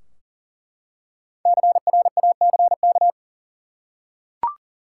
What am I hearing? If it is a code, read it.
Morse code, heard as a keyed tone, about 40 wpm, 700 Hz. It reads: CRACK
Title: Five Letter Words - 40wpm